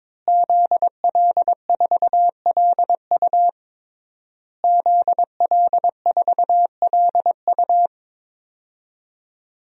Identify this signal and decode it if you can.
Morse code, heard as a keyed tone, about 22 wpm, 700 Hz. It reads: ZL4LU ZL4LU
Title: DXCC-100 Call Signs - 5 Characters - Repeated 3x - 22wpm